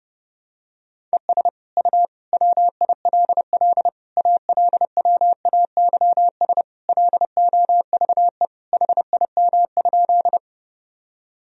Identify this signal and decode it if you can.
Morse code, heard as a keyed tone, about 32 wpm, 700 Hz. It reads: ES U WILL ALWAYS LOVE HIM?
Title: Sentences from Top 400 Words and Abbreviations 30wpm